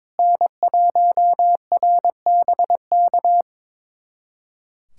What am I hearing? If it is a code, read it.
Morse code, heard as a keyed tone, about 22 wpm, 700 Hz. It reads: N1RBK
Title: USA Callsign DE Callsign - 5 Characters - 22wpm